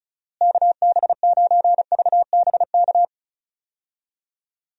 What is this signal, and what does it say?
Morse code, heard as a keyed tone, about 36 wpm, 700 Hz. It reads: KB9VBK
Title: USA Call Signs - 6 Characters - Repeated 2x - 35wpm